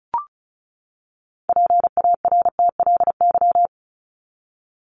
Morse code, heard as a keyed tone, about 34 wpm, 700 Hz. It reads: PARTLY
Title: LY Suffix - Top 100 Words with Base Words 35wpm